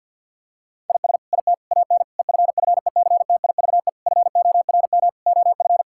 Morse code, heard as a keyed tone, about 50 wpm, 700 Hz. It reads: IS IT AN EFFECTIVE FORM OF
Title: Sentences from Top 1000 Words - Review Part 6 - 50wpm